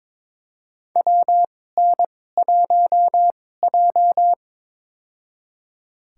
Morse code, heard as a keyed tone, about 23 wpm, spaced as for 16 wpm, 700 Hz. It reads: WN1J
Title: USA Call Signs - 4 Characters - 2x Character Spacing - 22wpm